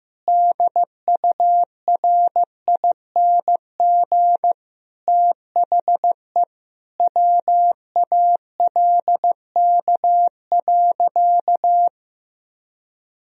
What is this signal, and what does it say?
Morse code, heard as a keyed tone, about 15 wpm, 700 Hz. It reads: DURING THE WALK.
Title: Sentences from Top 1000 Words - Review Part 2 - 15wpm